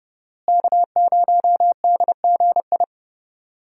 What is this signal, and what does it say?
Morse code, heard as a keyed tone, about 30 wpm, 700 Hz. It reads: K0DGI
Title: USA Call Signs - 5 Characters - Repeated 2x - 30wpm